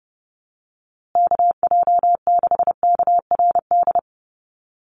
Morse code, heard as a keyed tone, about 30 wpm, 700 Hz. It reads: KJ6KRD
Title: USA Call Signs Repeated 2x 30wpm